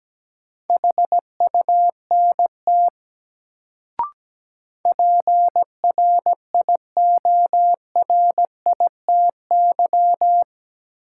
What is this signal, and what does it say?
Morse code, heard as a keyed tone, about 17 wpm, 700 Hz. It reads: HUNT PRIORITY
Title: Sentences from Top 1600 Words - New Word Drill - 17wpm